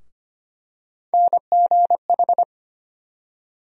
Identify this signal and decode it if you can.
Morse code, heard as a keyed tone, about 25 wpm, 700 Hz. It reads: NGH
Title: USA Call Sign Postfixes - 3 Letters 25wpm